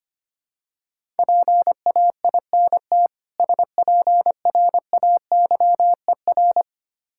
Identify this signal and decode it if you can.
Morse code, heard as a keyed tone, about 25 wpm, 700 Hz. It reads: PAINT SPRAYER
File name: practice / 25wpm / Difficult Words with Hints - 25wpm